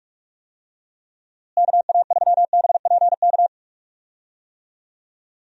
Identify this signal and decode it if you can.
Morse code, heard as a keyed tone, about 46 wpm, 700 Hz. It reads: KA3BPK
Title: USA Call Signs - 6 Characters - Repeated 3x - 45wpm